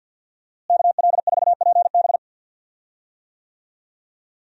Morse code, heard as a keyed tone, about 50 wpm, 700 Hz. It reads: KL4PB
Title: USA Call Signs - 5 Characters - Repeated 3x - 50wpm